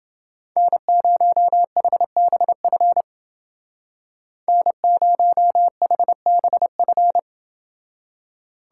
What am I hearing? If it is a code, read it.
Morse code, heard as a keyed tone, about 28 wpm, 700 Hz. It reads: N0HBF N0HBF